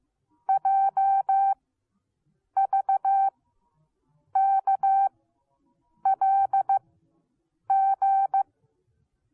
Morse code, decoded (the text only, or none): JVKLG